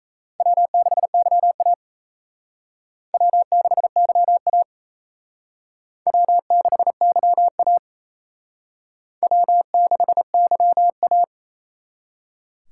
Morse code, heard as a keyed tone, about 35 wpm, 700 Hz. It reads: W6YA W6YA W6YA W6YA
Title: Speed Racing - USA 1x2 Call Signs 28wpm